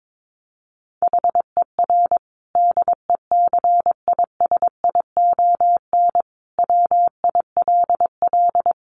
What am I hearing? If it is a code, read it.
Morse code, heard as a keyed tone, about 22 wpm, 700 Hz. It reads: HER DECISION WILL
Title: Sentences from Top 1300 Words 22wpm